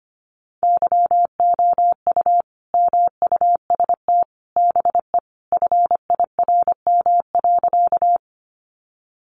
Morse code, heard as a keyed tone, about 25 wpm, 700 Hz. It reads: YOU MUST BE FIRM.
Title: Sentences from Top 1400 Words 25wpm